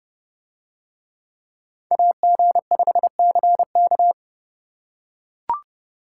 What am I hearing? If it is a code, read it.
Morse code, heard as a keyed tone, about 28 wpm, 700 Hz. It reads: AG5CK